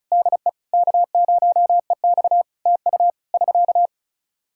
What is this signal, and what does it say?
Morse code, heard as a keyed tone, about 34 wpm, 700 Hz. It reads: DE K0EX TU <SK>
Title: K1USN SST QSO Practice 35wpm